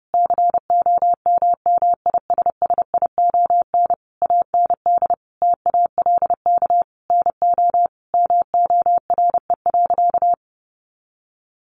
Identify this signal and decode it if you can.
Morse code, heard as a keyed tone, about 30 wpm, 700 Hz. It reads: COMMISSION AND TALK NO MORE.
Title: Sentences from Top 1400 Words 30wpm